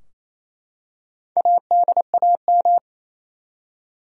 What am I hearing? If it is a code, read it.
Morse code, heard as a keyed tone, about 27 wpm, 700 Hz. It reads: ADAM
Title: USA Common Names - Max 5 Letters - 28wpm